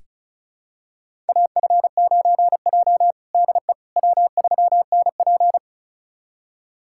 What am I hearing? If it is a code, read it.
Morse code, heard as a keyed tone, about 36 wpm, 700 Hz. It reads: AF9J DE W3NP
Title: USA Callsign DE Callsign - 4 Characters - 35wpm